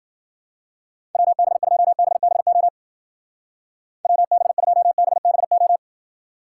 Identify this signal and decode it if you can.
Morse code, heard as a keyed tone, about 50 wpm, 700 Hz. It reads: WB2BBO WB2BBO
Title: Speed Racing - USA 2x3 Call Signs 40wpm